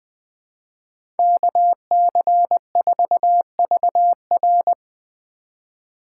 Morse code, heard as a keyed tone, about 20 wpm, 700 Hz. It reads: KC4VR
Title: USA Call Signs - 5 Characters - 3x Character Spacing - 20wpm